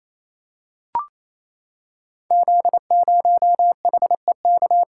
Morse code, heard as a keyed tone, about 28 wpm, 700 Hz. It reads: Z0HEK
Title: Code Groups - 5 Letter:Numbers - 28wpm